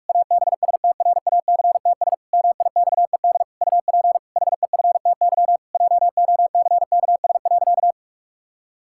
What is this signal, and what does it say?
Morse code, heard as a keyed tone, about 41 wpm, 700 Hz. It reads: ABSTRACTS MIXED UP HEFTY JOCKS.
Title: Pangram Challenge v2 - 45wpm